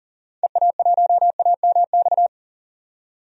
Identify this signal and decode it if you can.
Morse code, heard as a keyed tone, about 37 wpm, 700 Hz. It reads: EA1AMX
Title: DXCC-100 Call Signs - 6 Characters - 40wpm